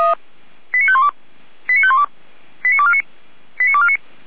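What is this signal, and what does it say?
Five-tone selective call (ZVEI), heard as quick runs of five tones, tones of about 70 ms. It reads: 89512 89512 89249 89249
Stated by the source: ZVEI five-tone selective call 89229, received by SDR.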